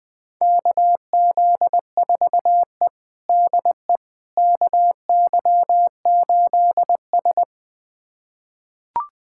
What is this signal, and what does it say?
Morse code, heard as a keyed tone, about 20 wpm, 700 Hz. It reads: KZ4E DE KY8S
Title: USA Callsign DE Callsign - 4 Characters - 20wpm